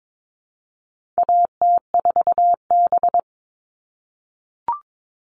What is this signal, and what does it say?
Morse code, heard as a keyed tone, about 22 wpm, 700 Hz. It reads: AT4B
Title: All Cut Numbers - 4-Digit 22wpm